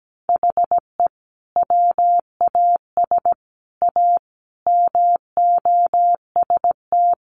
Morse code, heard as a keyed tone, about 17 wpm, 700 Hz. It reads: HE WAS A MOST
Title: Sentences from Top 1200 Words 17wpm